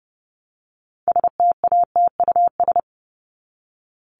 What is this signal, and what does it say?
Morse code, heard as a keyed tone, about 30 wpm, 700 Hz. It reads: STATUS
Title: Top 1251-1500 Words 30wpm